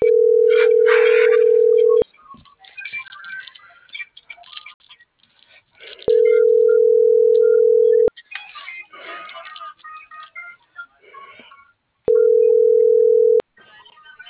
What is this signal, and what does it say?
Ringback tone (North America), 440 + 480 Hz.